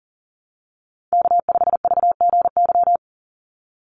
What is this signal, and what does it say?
Morse code, heard as a keyed tone, about 40 wpm, 700 Hz. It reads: K5VGY
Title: USA Call Signs 40wpm